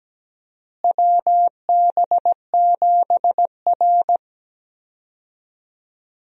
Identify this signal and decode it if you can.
Morse code, heard as a keyed tone, about 17 wpm, 700 Hz. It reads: WB7R
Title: USA Call Signs - 4 Characters - Repeated 2x - 17wpm